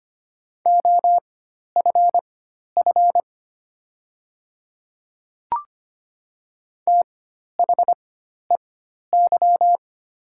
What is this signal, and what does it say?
Morse code, heard as a keyed tone, about 25 wpm, spaced as for 12 wpm, 690 Hz. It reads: OFF THEY
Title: 4x Character Spacing - Top 300 Words - 25wpm